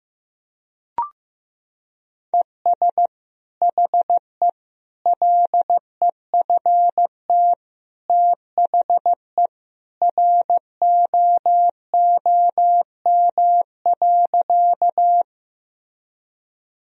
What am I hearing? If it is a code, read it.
Morse code, heard as a keyed tone, about 15 wpm, 700 Hz. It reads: ES HE LEFT THE ROOM.
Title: Sentences from Top 300 Words and Abbreviations 15wpm